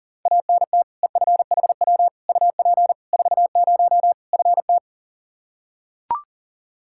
Morse code, heard as a keyed tone, about 40 wpm, 700 Hz. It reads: ANT EFHW UP 40 FT
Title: Practice QSO - Exchange 2 - Info - 40wpm